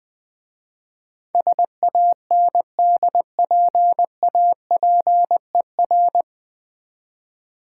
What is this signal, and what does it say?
Morse code, heard as a keyed tone, about 20 wpm, 700 Hz. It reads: SANDPAPER
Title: Difficult Words with Hints - 20wpm